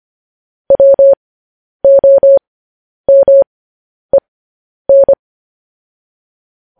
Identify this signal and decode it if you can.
Morse code, heard as a keyed tone, about 25 wpm, spaced as for 10 wpm, 550 Hz. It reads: WOMEN
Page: news-headlines-in-morse-code-at-10-wpm